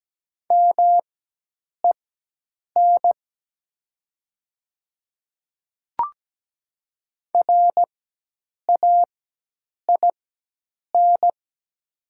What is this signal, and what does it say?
Morse code, heard as a keyed tone, about 17 wpm, spaced as for 8 wpm, 700 Hz. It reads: MEN RAIN
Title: 4x Character Spacing - Top 300 Words - 17wpm